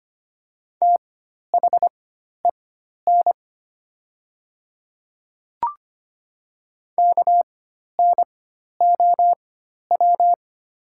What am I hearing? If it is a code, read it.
Morse code, heard as a keyed tone, about 25 wpm, spaced as for 12 wpm, 700 Hz. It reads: THEN KNOW
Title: 4x Character Spacing - Top 100 Words - 25wpm